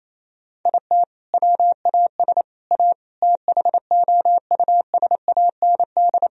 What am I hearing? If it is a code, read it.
Morse code, heard as a keyed tone, about 28 wpm, 700 Hz. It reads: IT WAS A THOUSAND
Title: Sentences from Top 1000 Words - Review Part 3 - 28wpm